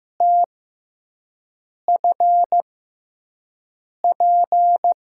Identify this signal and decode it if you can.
Morse code, heard as a keyed tone, about 15 wpm, spaced as for 5 wpm, 700 Hz. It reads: TFP
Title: USA Call Signs - 6 Characters - 6x Character Spacing - 15wpm